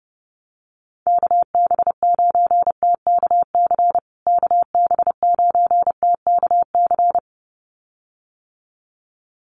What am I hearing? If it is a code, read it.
Morse code, heard as a keyed tone, about 30 wpm, 700 Hz. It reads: KB9TKC KB9TKC
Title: USA Call Signs Repeated 2x 30wpm